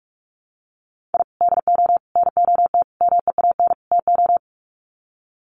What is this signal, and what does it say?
Morse code, heard as a keyed tone, about 44 wpm, 700 Hz. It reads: I DO NOT MEAN TO
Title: Sets of 5 Words 45wpm